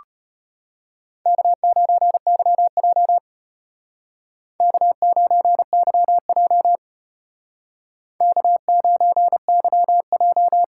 Morse code, heard as a keyed tone, about 31 wpm, 700 Hz. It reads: K9YJ K9YJ K9YJ
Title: Speed Racing - USA 1x2 Call Signs 25wpm